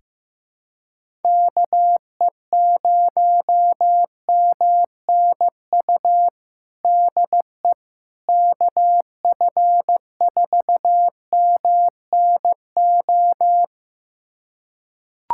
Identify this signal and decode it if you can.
Morse code, heard as a keyed tone, about 15 wpm, 700 Hz. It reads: KE0MNU DE KF4MNO
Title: USA Callsign DE Callsign - 6 Characters - 15wpm